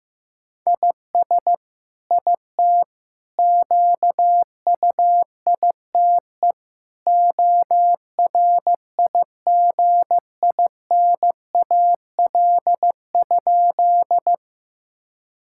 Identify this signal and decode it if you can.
Morse code, heard as a keyed tone, about 14 wpm, 700 Hz. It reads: IS IT QUITE ORIGINAL?
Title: Sentences from Top 1000 Words - Review Part 6 - 15wpm